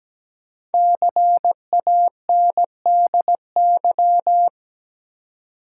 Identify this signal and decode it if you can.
Morse code, heard as a keyed tone, about 17 wpm, 700 Hz. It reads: CANDY
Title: Five Letter Words - 17wpm